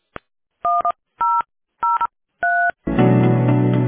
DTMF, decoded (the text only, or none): 1003